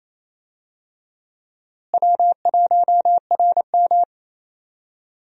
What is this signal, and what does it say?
Morse code, heard as a keyed tone, about 28 wpm, 700 Hz. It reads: W1RM
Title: USA Call Signs - 4 Characters - Repeated 3x - 28wpm